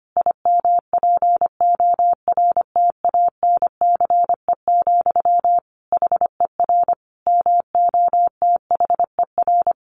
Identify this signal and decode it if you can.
Morse code, heard as a keyed tone, about 25 wpm, 700 Hz. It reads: IMPORTANCE, HER MOTHER
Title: Sentences from Top 1500 Words 25wpm